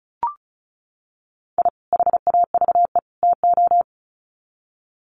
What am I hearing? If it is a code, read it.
Morse code, heard as a keyed tone, about 34 wpm, 700 Hz. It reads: I HAVE TO
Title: Sets of 3 Words 35wpm